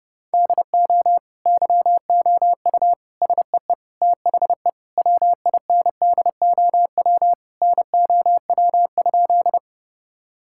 Morse code, heard as a keyed tone, about 30 wpm, 700 Hz. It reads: DO YOU SEE THE WINDOW NOW?
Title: Sentences from Top 1600 Words 30wpm